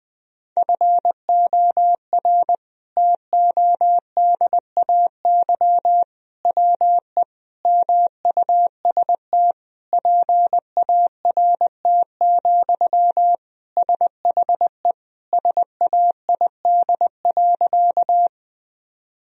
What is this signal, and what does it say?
Morse code, heard as a keyed tone, about 20 wpm, 700 Hz. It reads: FOR TODAY WE MUST PART, SHE SAID.
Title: Sentences from Top 1000 Words - Review Part 1 - 20wpm